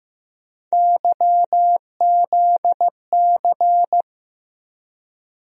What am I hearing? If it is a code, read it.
Morse code, heard as a keyed tone, about 15 wpm, 700 Hz. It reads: YZC
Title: USA Call Sign Postfixes - 3 Letters 15wpm